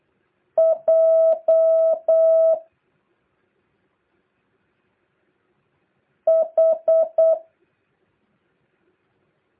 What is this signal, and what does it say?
Morse code, heard as a keyed tone, about 8 wpm, 650 Hz. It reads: J H
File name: jh8cbh / 2023cw